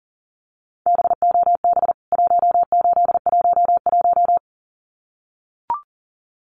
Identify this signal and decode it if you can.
Morse code, heard as a keyed tone, about 40 wpm, 700 Hz. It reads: BOB 1811
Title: CWT Exchange 40wpm